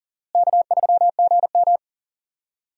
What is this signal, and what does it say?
Morse code, heard as a keyed tone, about 40 wpm, 700 Hz. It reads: K3GM
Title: USA Call Signs - 4 Characters - Repeated 3x - 40wpm